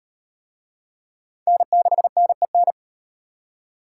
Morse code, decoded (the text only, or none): N6NEN